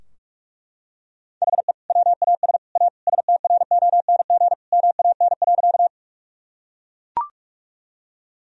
Morse code, heard as a keyed tone, about 41 wpm, 700 Hz. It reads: HE WAS A STRONG MAN.
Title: Sentences from Top 1000 Words - Review Part 3 - 45wpm